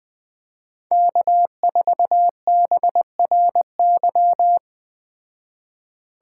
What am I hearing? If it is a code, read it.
Morse code, heard as a keyed tone, about 20 wpm, 700 Hz. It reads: K4BRY
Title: Speed Racing - USA Call Signs 17wpm